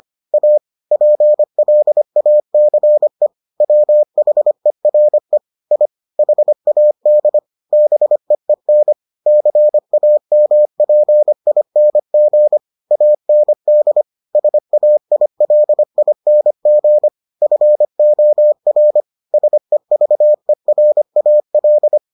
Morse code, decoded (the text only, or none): A PLACE WHERE I HAD BEEN CAMPING AND SAILING FOR SEVERAL